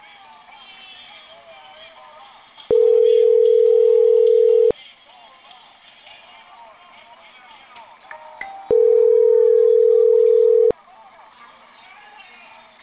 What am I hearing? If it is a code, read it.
Ringback tone (North America), 440 + 480 Hz.